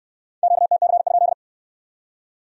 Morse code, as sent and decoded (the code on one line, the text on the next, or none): --.- - -.-. ..--..
QTC?